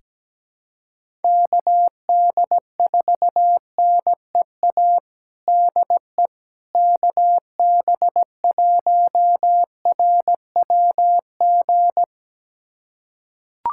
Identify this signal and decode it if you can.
Morse code, heard as a keyed tone, about 17 wpm, 700 Hz. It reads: KD4NEA DE KB1RWG
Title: USA Callsign DE Callsign - 6 Characters - 17wpm